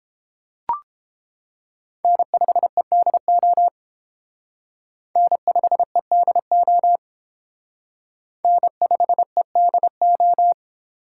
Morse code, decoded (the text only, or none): N5EDO N5EDO N5EDO